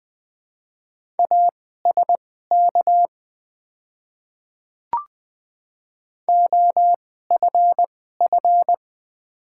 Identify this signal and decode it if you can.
Morse code, heard as a keyed tone, about 20 wpm, spaced as for 15 wpm, 700 Hz. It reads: ASK OFF
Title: 2x Character Spacing - Top 300 Words - 20wpm